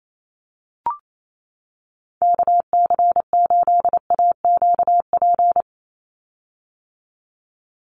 Morse code, heard as a keyed tone, about 28 wpm, 700 Hz. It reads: KC8AQP
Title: USA Call Signs 28wpm